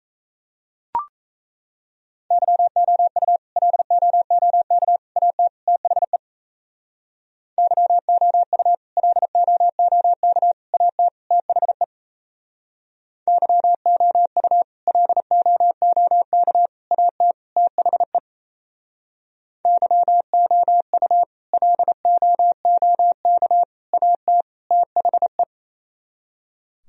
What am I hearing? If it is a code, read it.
Morse code, heard as a keyed tone, about 36 wpm, 700 Hz. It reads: YOU LOOK AT THE YOU LOOK AT THE YOU LOOK AT THE YOU LOOK AT THE